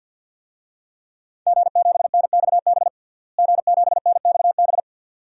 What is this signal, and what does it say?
Morse code, heard as a keyed tone, about 50 wpm, 700 Hz. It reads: G7NXB G7NXB
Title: DXCC-100 Call Signs - 5 Characters - Repeated 2x - 50wpm